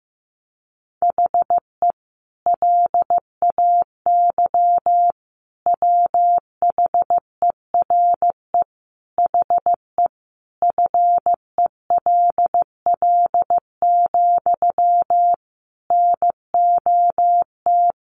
Morse code, heard as a keyed tone, about 15 wpm, 700 Hz. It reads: HE LAY WHERE HE FELL, NOT
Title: A Sound Of Thunder 15wpm